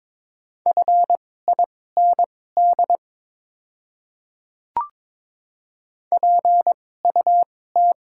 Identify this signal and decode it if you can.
Morse code, heard as a keyed tone, about 22 wpm, spaced as for 16 wpm, 700 Hz. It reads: FIND PUT
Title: 2x Character Spacing - Top 500 Words - 22wpm